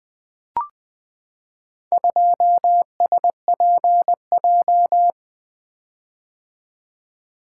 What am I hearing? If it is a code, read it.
Morse code, heard as a keyed tone, about 20 wpm, 700 Hz. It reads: 2SPJ
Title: Four Letter-Number Combinations - 20wpm